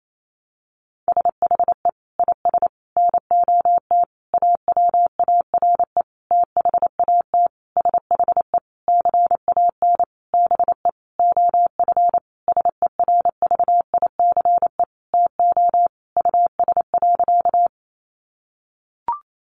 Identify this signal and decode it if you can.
Morse code, heard as a keyed tone, about 28 wpm, 700 Hz. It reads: SHE IS NOT AWARE THAT SHE CAN BE OF SERVICE TO US.
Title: Sentences from Top 1200 Words 28wpm